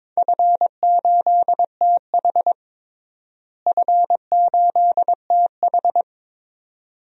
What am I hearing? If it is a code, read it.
Morse code, heard as a keyed tone, about 22 wpm, 700 Hz. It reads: F8TH F8TH
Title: DXCC-100 Call Signs - 4 Characters - Repeated 2x - 22wpm